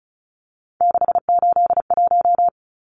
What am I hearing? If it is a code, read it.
Morse code, heard as a keyed tone, about 34 wpm, 700 Hz. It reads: 681
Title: Numbers - 3-Digit 35wpm